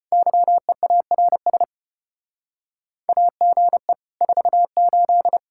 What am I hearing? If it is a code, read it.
Morse code, heard as a keyed tone, about 33 wpm, 700 Hz. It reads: YEARS AGE 48